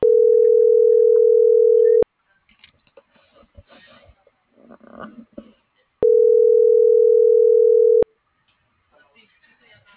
Ringback tone (North America), 440 + 480 Hz.